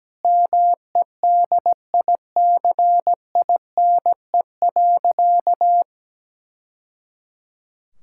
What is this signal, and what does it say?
Morse code, heard as a keyed tone, about 17 wpm, 700 Hz. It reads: MEDICINE.
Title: Sentences from Top 1000 Words - Review Part 6 - 17wpm